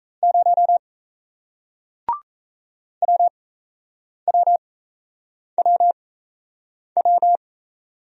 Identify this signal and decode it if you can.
Morse code, heard as a keyed tone, about 37 wpm, spaced as for 8 wpm, 700 Hz. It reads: 0 WWWW